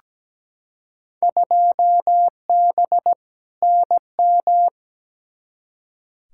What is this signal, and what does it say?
Morse code, heard as a keyed tone, about 17 wpm, 700 Hz. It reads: 2B NM
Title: Field Day Exchanges Encore - 17wpm